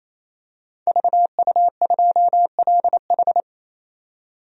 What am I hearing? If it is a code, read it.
Morse code, heard as a keyed tone, about 28 wpm, 700 Hz. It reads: VU2LH
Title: DXCC-100 Call Signs - 5 Characters - 28wpm